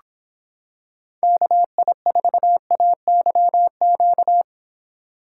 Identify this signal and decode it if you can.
Morse code, heard as a keyed tone, about 26 wpm, 700 Hz. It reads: KI4AYQ